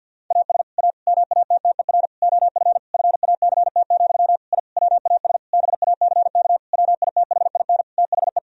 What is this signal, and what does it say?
Morse code, heard as a keyed tone, about 51 wpm, 700 Hz. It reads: AS A MATTER OF FACT, I WAS BACK WITHIN THE
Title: Sentences from Top 1000 Words - Review Part 4 - 50wpm